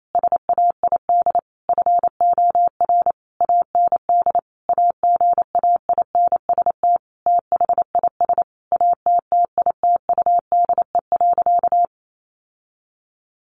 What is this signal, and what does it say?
Morse code, heard as a keyed tone, about 28 wpm, 700 Hz. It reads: SAID FOR AND AGAINST THIS ATTITUDE.
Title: Sentences from Top 1300 Words 28wpm